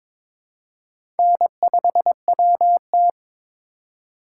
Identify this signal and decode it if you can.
Morse code, heard as a keyed tone, about 23 wpm, 700 Hz. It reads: N5WT
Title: USA Call Signs - 4 Characters - 4x Character Spacing - 22wpm